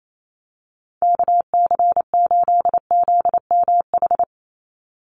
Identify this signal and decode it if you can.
Morse code, heard as a keyed tone, about 28 wpm, 700 Hz. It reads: KC8ZMH
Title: USA Call Signs 28wpm